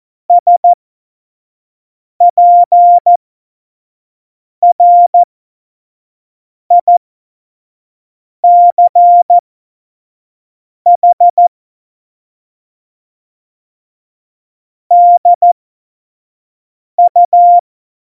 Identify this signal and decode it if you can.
Morse code, heard as a keyed tone, about 14 wpm, spaced as for 5 wpm, 700 Hz. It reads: SPRICH DU
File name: Tempo 25 F70